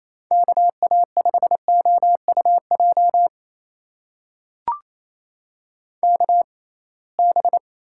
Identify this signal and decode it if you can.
Morse code, heard as a keyed tone, about 28 wpm, spaced as for 21 wpm, 700 Hz. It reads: KA5OUJ KB
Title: USA Call Signs - 6 Characters - 6x Character Spacing - 28wpm